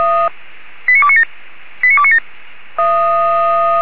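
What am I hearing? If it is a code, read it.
Five-tone selective call (ZVEI), heard as quick runs of five tones, tones of about 70 ms. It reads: 89297 89297
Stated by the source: ZVEI five-tone selective call received by SDR, feuerwehr-orlamuende.org (fire brigade).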